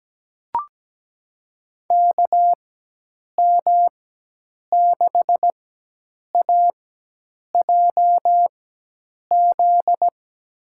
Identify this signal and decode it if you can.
Morse code, heard as a keyed tone, about 17 wpm, spaced as for 8 wpm, 700 Hz. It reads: KM6AJZ
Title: USA Call Signs - 6 Characters - 4x Character Spacing - 17wpm